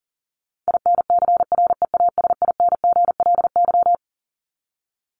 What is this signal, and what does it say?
Morse code, heard as a keyed tone, about 40 wpm, 700 Hz. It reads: INCREASINGLY
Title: LY Suffix - Top 100 Words with Base Words 40wpm